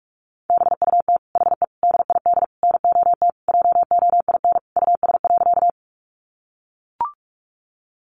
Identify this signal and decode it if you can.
Morse code, heard as a keyed tone, about 44 wpm, 700 Hz. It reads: BUT HE DID NOT JOIN US.
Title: Sentences from Top 1400 Words 45wpm